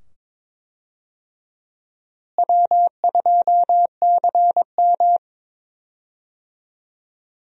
Morse code, heard as a keyed tone, about 22 wpm, 700 Hz. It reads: W2CM
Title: USA Call Signs - 4 Characters - Repeated 2x - 22wpm